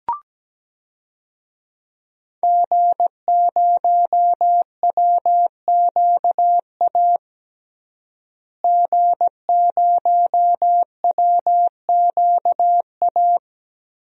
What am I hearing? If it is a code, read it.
Morse code, heard as a keyed tone, about 17 wpm, 700 Hz. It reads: G0WQA G0WQA
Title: DXCC-100 Call Signs - 5 Characters - Repeated 2x - 17wpm